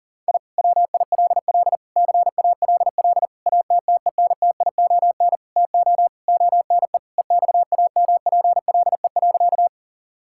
Morse code, heard as a keyed tone, about 40 wpm, 700 Hz. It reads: I WILL CALL ATTENTION TO ONE EXAMPLE.
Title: Sentences from Top 1000 Words - Review Part 6 - 40wpm